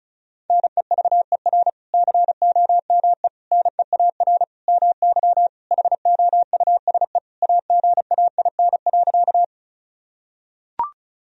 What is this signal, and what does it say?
Morse code, heard as a keyed tone, about 36 wpm, 700 Hz. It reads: NEVER COME NEAR MY HOUSE AGAIN.
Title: Sentences from Top 1000 Words - Review Part 1 - 35wpm